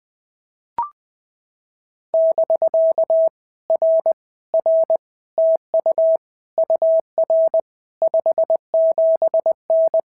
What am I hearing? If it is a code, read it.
Morse code, heard as a keyed tone, about 19 wpm, 650 Hz. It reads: <BK> R R TU UR 57N
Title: POTA QSO Practice - 20wpm